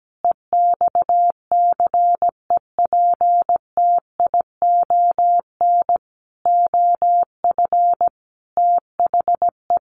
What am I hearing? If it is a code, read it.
Morse code, heard as a keyed tone, about 17 wpm, 700 Hz. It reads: EXCEPTION OF THE
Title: Sets of 5 Words Encore 17wpm